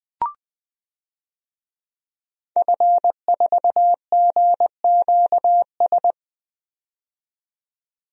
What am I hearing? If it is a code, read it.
Morse code, heard as a keyed tone, about 21 wpm, 700 Hz. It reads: F4GQS
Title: DXCC-100 Call Signs - 5 Characters - 20wpm